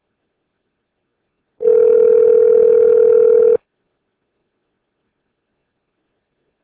Ringback tone (North America), 440 + 480 Hz.